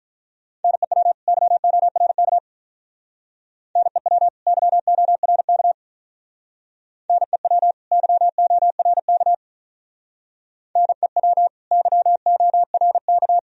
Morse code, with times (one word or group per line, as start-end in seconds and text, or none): 0.6-1.1 NEW
1.3-2.4 YORK
3.8-4.3 NEW
4.5-5.7 YORK
7.1-7.7 NEW
7.9-9.3 YORK
10.8-11.5 NEW
11.7-13.4 YORK